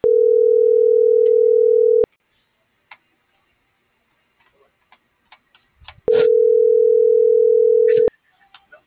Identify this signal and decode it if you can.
Ringback tone (North America), 440 + 480 Hz.